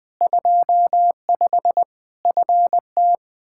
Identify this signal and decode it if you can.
Morse code, heard as a keyed tone, about 20 wpm, 700 Hz. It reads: 25 FT
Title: Practice QSO - Exchange 2 - Info - 20wpm